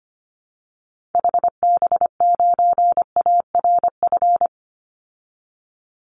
Morse code, heard as a keyed tone, about 25 wpm, 700 Hz. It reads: HB9ARF
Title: Call Signs Repeated 3x - 25wpm